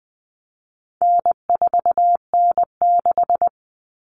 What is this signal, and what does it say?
Morse code, heard as a keyed tone, about 20 wpm, 700 Hz. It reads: N4N6
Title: Common Cut Numbers - 4-Digit 20wpm